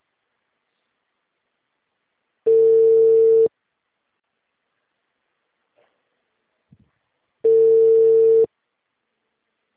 Ringback tone (China, 450 Hz).
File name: out- -2002- - - .34